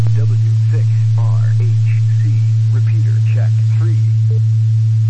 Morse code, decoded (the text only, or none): none